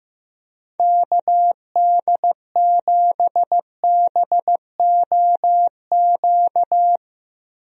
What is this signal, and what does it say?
Morse code, heard as a keyed tone, about 15 wpm, 700 Hz. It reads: KD7BOQ
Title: USA Call Signs - 6 Characters - Repeated 3x - 15wpm